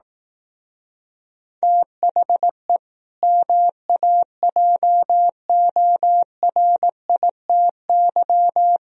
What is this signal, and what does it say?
Morse code, heard as a keyed tone, about 18 wpm, 700 Hz. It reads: THE MAJORITY
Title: Speed-Racing - Sets of 3 Words 15wpm